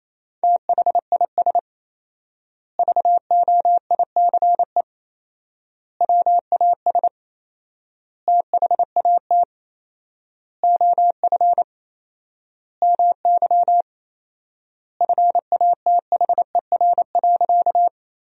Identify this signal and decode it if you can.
Morse code, heard as a keyed tone, about 28 wpm, 700 Hz. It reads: THIS VOICE WAS THAT OF MY FATHER.
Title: 4x Word-Spacing - Sentences from Top 500 Words - 28wpm